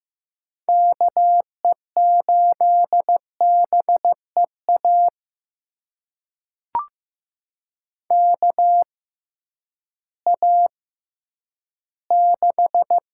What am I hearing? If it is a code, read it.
Morse code, heard as a keyed tone, about 15 wpm, spaced as for 10 wpm, 700 Hz. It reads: KE8BEA KA6
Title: USA Call Signs - 6 Characters - 6x Character Spacing - 15wpm